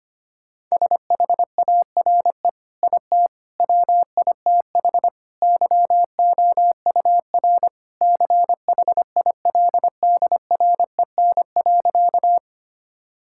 Morse code, heard as a keyed tone, about 25 wpm, 700 Hz. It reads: SHARE IT WITH YOUR CHILDREN.
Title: Sentences from Top 1000 Words - Review Part 6 - 25wpm